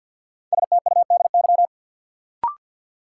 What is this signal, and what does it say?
Morse code, heard as a keyed tone, about 52 wpm, 700 Hz. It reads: STUDY
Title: Sentences from Top 300 Words - New Word Drill - 50wpm